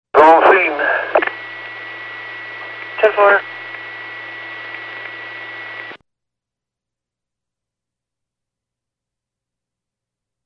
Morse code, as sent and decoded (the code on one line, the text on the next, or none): none